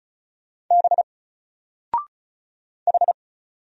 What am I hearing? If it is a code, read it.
Morse code, heard as a keyed tone, about 36 wpm, 700 Hz. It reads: B H
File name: Single-Letter-Number-35wpm